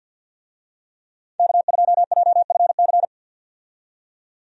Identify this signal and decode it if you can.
Morse code, heard as a keyed tone, about 53 wpm, 700 Hz. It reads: K2JFC
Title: USA Call Signs - 5 Characters - 50wpm